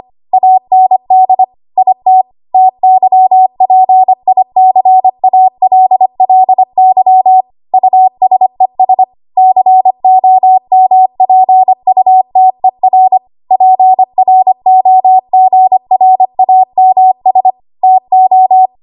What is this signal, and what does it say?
Morse code, heard as a keyed tone, about 25 wpm, 750 Hz. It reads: AND IT TYPICALLY USES COMPUTER PROGRAMS TO
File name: Morse / 25WPM